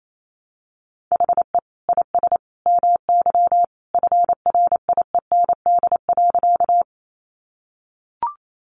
Morse code, heard as a keyed tone, about 28 wpm, 700 Hz. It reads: HE IS MY FRIEND.
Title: Sentences from Top 300 Words 28wpm